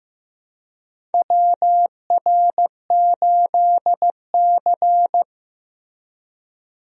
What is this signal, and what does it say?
Morse code, heard as a keyed tone, about 15 wpm, 700 Hz. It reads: WR8C
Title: USA Call Signs - 4 Characters - Repeated 2x - 15wpm